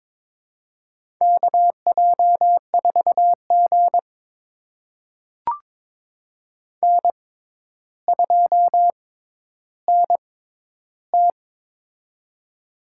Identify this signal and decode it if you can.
Morse code, heard as a keyed tone, about 22 wpm, spaced as for 11 wpm, 700 Hz. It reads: KJ4G N2NT
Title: USA Call Signs - 4 Characters - 6x Character Spacing - 22wpm